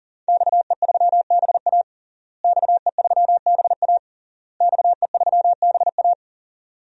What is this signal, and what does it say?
Morse code, heard as a keyed tone, about 40 wpm, 700 Hz. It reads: XE3BA XE3BA XE3BA
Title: DXCC-100 Call Signs - 5 Characters - Repeated 3x - 40wpm